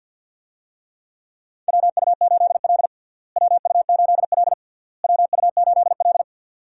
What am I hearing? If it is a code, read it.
Morse code, heard as a keyed tone, about 53 wpm, 700 Hz. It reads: WU8L WU8L WU8L
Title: USA Call Signs - 4 Characters - Repeated 3x - 50wpm